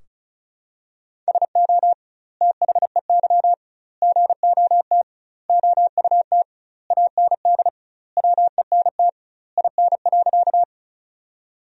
Morse code, heard as a keyed tone, about 36 wpm, 700 Hz. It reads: SO THEY GOT OUT AND WENT IN.
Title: 2x Word-Spacing - Sentences from Top 300 Words - 35wpm